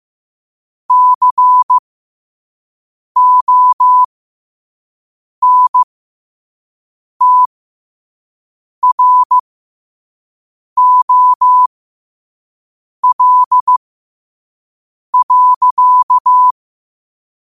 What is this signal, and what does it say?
Morse code, heard as a keyed tone, about 15 wpm, spaced as for 5 wpm, 1000 Hz. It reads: CONTROL.
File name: qotdfiles / 5wpm